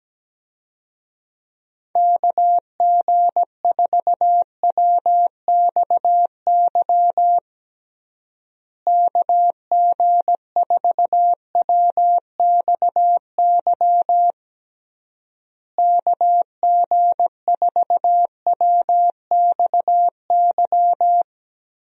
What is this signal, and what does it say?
Morse code, heard as a keyed tone, about 17 wpm, 700 Hz. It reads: KG4WXY KG4WXY KG4WXY
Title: USA Call Signs - 6 Characters - Repeated 3x - 17wpm